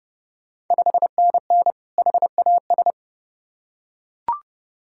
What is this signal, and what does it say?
Morse code, heard as a keyed tone, about 30 wpm, 700 Hz. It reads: 5NN HAS
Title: Ontario QSO Party Exchange 30wpm